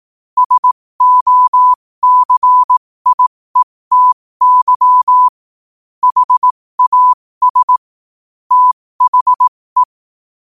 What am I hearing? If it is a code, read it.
Morse code, heard as a keyed tone, about 18 wpm, spaced as for 15 wpm, 1000 Hz. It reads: SOCIETY HAS THE
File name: 15wpm